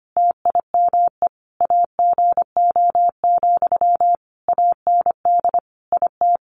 Morse code, heard as a keyed tone, about 25 wpm, 700 Hz. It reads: TIME AGO, AND IT
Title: Sentences from Top 600 Words 25wpm